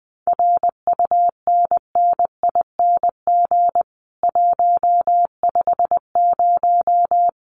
Morse code, heard as a keyed tone, about 20 wpm, 700 Hz. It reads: RUNNING 150
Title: Common QSO Phrases 20wpm